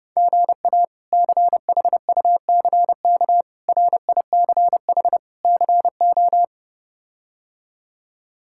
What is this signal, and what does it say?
Morse code, heard as a keyed tone, about 30 wpm, 700 Hz. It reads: GA CHUCK RICH CO
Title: K1USN SST QSO Practice - 30wpm